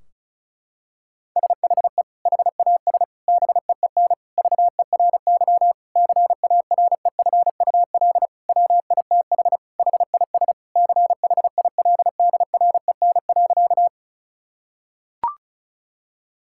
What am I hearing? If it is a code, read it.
Morse code, heard as a keyed tone, about 34 wpm, 700 Hz. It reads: SHE HAS BEEN VERY CAREFUL WITH HIS CHILDREN.
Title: Sentences from Top 1600 Words 35wpm